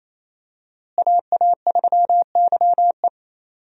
Morse code, heard as a keyed tone, about 28 wpm, 700 Hz. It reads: AA3YE
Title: USA Call Signs - 5 Characters - Repeated 3x - 28wpm